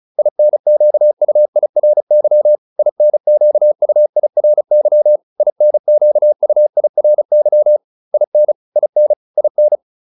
Morse code, heard as a keyed tone, about 34 wpm, 600 Hz. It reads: INQUIRY INQUIRY INQUIRY IN IN IN